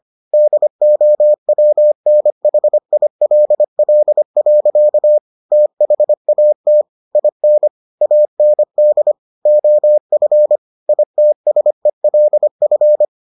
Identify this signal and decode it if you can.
Morse code, heard as a keyed tone, about 25 wpm, 600 Hz. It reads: DOWNHILL. THAT IN AND OF ITSELF